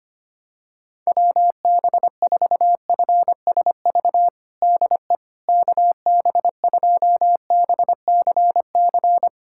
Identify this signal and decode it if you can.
Morse code, heard as a keyed tone, about 25 wpm, 700 Hz. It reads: WB4FSV DE KB2BCC
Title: USA Callsign DE Callsign - 6 Characters - 25wpm